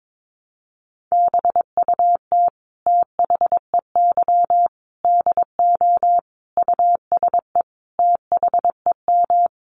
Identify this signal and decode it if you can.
Morse code, heard as a keyed tone, about 22 wpm, 700 Hz. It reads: BUT THEY DO USE THEM
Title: Sentences from Top 1300 Words 22wpm